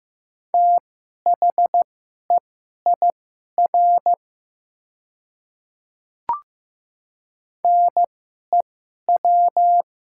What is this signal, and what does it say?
Morse code, heard as a keyed tone, about 15 wpm, spaced as for 11 wpm, 700 Hz. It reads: THEIR NEW
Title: 2x Character Spacing - Top 200 Words - 15wpm